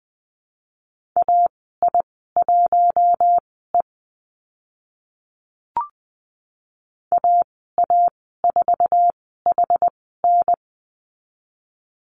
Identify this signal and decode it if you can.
Morse code, heard as a keyed tone, about 20 wpm, spaced as for 15 wpm, 700 Hz. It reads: AI1E AA4HN
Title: 062 Lesson 11g - Letter h - Call Signs 20wpm